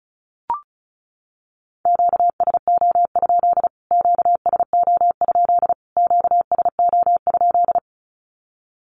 Morse code, heard as a keyed tone, about 34 wpm, 700 Hz. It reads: QSO? QSO? QSO?
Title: Q-Codes Repeated 3x 35wpm